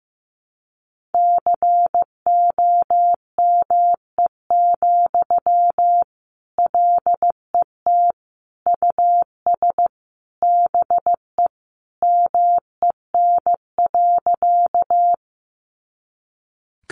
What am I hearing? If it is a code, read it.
Morse code, heard as a keyed tone, about 15 wpm, 700 Hz. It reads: COME, LET US BE MEN.
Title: Sentences from Top 300 Words 15wpm